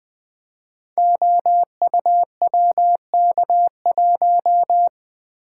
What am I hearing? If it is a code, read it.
Morse code, heard as a keyed tone, about 20 wpm, 700 Hz. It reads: OUWK1
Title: Code Groups - 5 Letter:Numbers - 20wpm